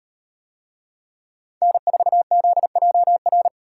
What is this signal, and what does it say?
Morse code, heard as a keyed tone, about 36 wpm, 700 Hz. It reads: N4ZJR